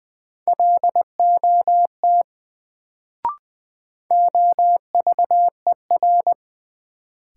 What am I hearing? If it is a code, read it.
Morse code, heard as a keyed tone, about 20 wpm, 700 Hz. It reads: LOT OVER
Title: Sets of 2 Words - New Word Drill - 20wpm